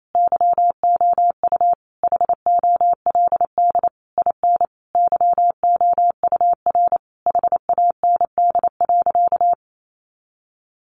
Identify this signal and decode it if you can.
Morse code, heard as a keyed tone, about 28 wpm, 700 Hz. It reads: YOU HOLD IN YOUR HAND.
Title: Sentences from Top 400 Words 28wpm